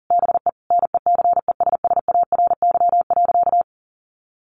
Morse code, heard as a keyed tone, about 40 wpm, 700 Hz. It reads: BE NECESSARY.
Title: Sentences from Top 800 Words 40wpm